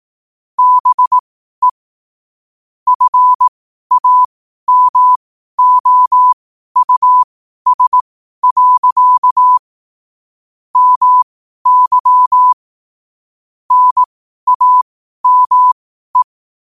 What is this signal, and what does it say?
Morse code, heard as a keyed tone, about 18 wpm, spaced as for 12 wpm, 1000 Hz. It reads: BE FAMOUS. MY NAME
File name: qotdfiles / 12wpm